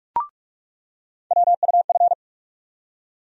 Morse code, heard as a keyed tone, about 44 wpm, 700 Hz. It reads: WUF
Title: USA Call Sign Postfixes - 3 Letters 45wpm